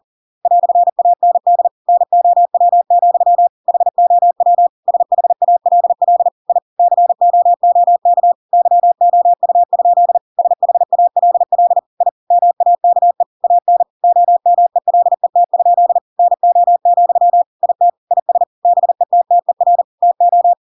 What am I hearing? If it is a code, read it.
Morse code, heard as a keyed tone, about 40 wpm, 700 Hz. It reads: "AND NOW, HOW SHALL I COOK YOU? SHALL I MAKE AN OMELET? NO, IT IS BETTER TO
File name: Pinocchio / Pinocchio ch5 40 40 ik1hge